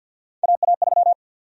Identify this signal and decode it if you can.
Morse code, heard as a keyed tone, about 52 wpm, 700 Hz. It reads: AA3
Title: USA Call Sign Prefixes - 50wpm